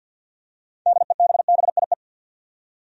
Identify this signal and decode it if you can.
Morse code, heard as a keyed tone, about 53 wpm, 700 Hz. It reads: DEBBIE
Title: USA Common Names - 50wpm